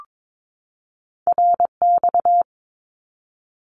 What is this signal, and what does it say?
Morse code, heard as a keyed tone, about 22 wpm, 700 Hz. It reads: RX
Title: Abbreviations 22wpm